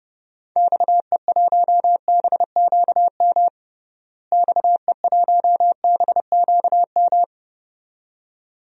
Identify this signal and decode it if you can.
Morse code, heard as a keyed tone, about 31 wpm, 700 Hz. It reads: XE1BQM XE1BQM
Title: DXCC-100 Call Signs - 6 Characters - Repeated 2x - 30wpm